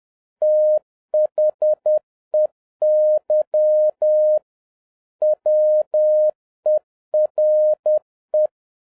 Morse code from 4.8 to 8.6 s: WERE